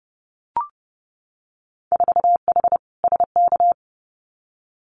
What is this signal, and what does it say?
Morse code, heard as a keyed tone, about 30 wpm, 700 Hz. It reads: 4H SK
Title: Winter Field Day Exchanges Encore 30wpm